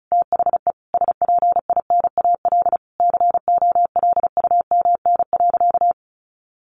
Morse code, heard as a keyed tone, about 34 wpm, 700 Hz. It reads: THE SPINAL COLUMN.
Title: A Sound Of Thunder 35wpm